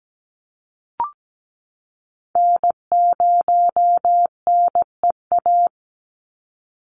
Morse code, heard as a keyed tone, about 17 wpm, 700 Hz. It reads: N0NEA
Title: USA Call Signs 17wpm